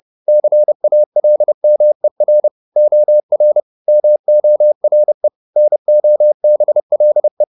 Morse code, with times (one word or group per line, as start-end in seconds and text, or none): 0.3-2.5 CALMER
2.8-3.6 OR
3.9-5.3 MORE
5.6-7.4 NOBLE